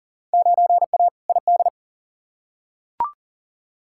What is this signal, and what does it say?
Morse code, heard as a keyed tone, about 40 wpm, 700 Hz. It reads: 9A ID
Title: Field Day Exchanges Encore - 40wpm